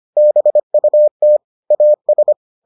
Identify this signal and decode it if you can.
Morse code, heard as a keyed tone, about 25 wpm, 600 Hz. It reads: BUT AS